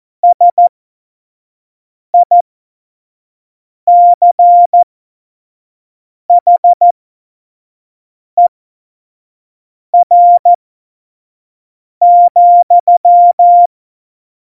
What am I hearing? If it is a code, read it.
Morse code, heard as a keyed tone, about 14 wpm, spaced as for 5 wpm, 700 Hz. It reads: SICHER,
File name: Morsekurs / Witze / Witz 26 Mai 2020, 25 F70